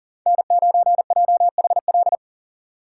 Morse code, heard as a keyed tone, about 40 wpm, 700 Hz. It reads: N9JHL
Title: K1USN SST QSO Practice - 40wpm